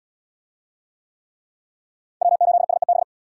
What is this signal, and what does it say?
Morse code, heard as a keyed tone, about 66 wpm, 700 Hz. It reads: W8SEB